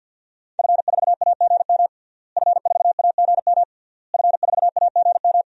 Morse code, heard as a keyed tone, about 48 wpm, 700 Hz. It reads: F4AGM F4AGM F4AGM
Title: DXCC-100 Call Signs - 5 Characters - Repeated 3x - 50wpm